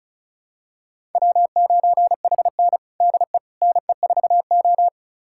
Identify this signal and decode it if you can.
Morse code, heard as a keyed tone, about 36 wpm, 700 Hz. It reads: W9HN DE NE4O
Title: USA Callsign DE Callsign - 4 Characters - 35wpm